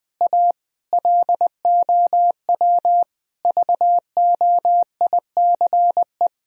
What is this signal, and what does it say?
Morse code, heard as a keyed tone, about 20 wpm, 700 Hz. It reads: A LOW VOICE
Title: Sentences from Top 1000 Words - Review Part 1 - 20wpm